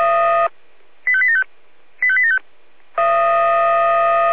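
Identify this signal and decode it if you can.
Five-tone selective call (ZVEI), heard as quick runs of five tones, tones of about 70 ms. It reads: 85875 85875
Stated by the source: ZVEI five-tone selective call received by SDR.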